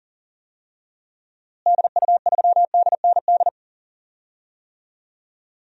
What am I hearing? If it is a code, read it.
Morse code, heard as a keyed tone, about 40 wpm, 700 Hz. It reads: DU3DND
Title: DXCC-100 Call Signs - 6 Characters - Repeated 2x - 40wpm